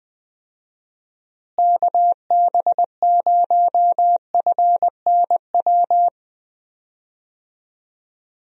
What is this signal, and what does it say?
Morse code, heard as a keyed tone, about 20 wpm, 700 Hz. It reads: KB0FNW
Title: USA Call Signs - 6 Characters - Repeated 2x - 20wpm